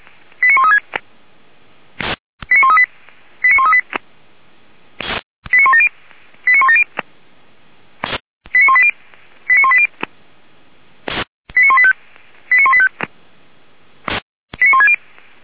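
Five-tone selective call (ZVEI), heard as quick runs of five tones, tones of about 70 ms. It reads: 89136 89138 89138 89170 89170 89180 89180 89185 89185 89160
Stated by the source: ZVEI five-tone selective call 89135 received on an SDR, from the Feuerwehr Orlamünde audio archive.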